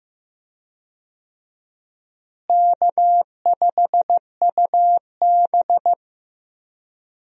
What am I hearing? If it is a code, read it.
Morse code, heard as a keyed tone, about 15 wpm, 700 Hz. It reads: K5UB
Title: USA Call Signs - 4 Characters - Repeated 2x - 15wpm